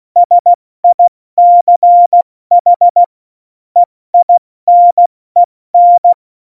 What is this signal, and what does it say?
Morse code, heard as a keyed tone, about 16 wpm, 700 Hz. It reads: SICH EINEN
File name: Morsekurs / Witze / Witz 15. August 17, Tempo 70 F80